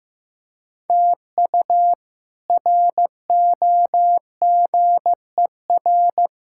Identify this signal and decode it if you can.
Morse code, heard as a keyed tone, about 15 wpm, 700 Hz. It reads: TU ROGER
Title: K1USN SST QSO Practice - 15wpm